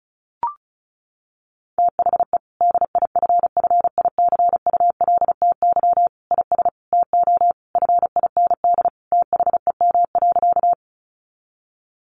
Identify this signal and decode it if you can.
Morse code, heard as a keyed tone, about 34 wpm, 700 Hz. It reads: THE DIFFICULTY IS TO FIND THEM.
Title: Sentences from Top 1500 Words 35wpm